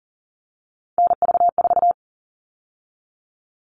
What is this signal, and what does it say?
Morse code, heard as a keyed tone, about 40 wpm, 700 Hz. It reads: NV4
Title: All Cut Numbers - 3-Digit 40wpm